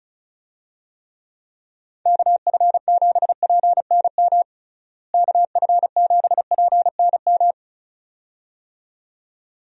Morse code, heard as a keyed tone, about 36 wpm, 700 Hz. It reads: KF7PNM KF7PNM
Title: USA Call Signs - 6 Characters - Repeated 2x - 35wpm